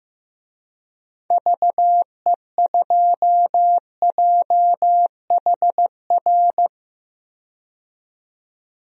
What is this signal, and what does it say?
Morse code, heard as a keyed tone, about 15 wpm, 700 Hz. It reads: VE2JHR
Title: DXCC-100 Call Signs - 6 Characters - 15wpm